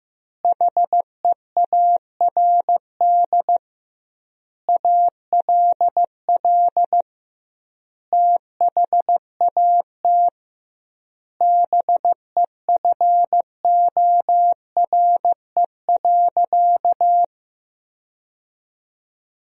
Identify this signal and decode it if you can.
Morse code, heard as a keyed tone, about 15 wpm, 700 Hz. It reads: HEARD ALL THAT BEFORE.
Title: 2x Word-Spacing - Sentences from Top 400 Words - 15wpm